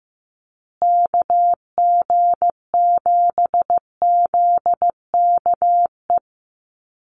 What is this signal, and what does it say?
Morse code, heard as a keyed tone, about 15 wpm, 700 Hz. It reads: KG7ZKE
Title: USA Call Signs Repeated 3x 15wpm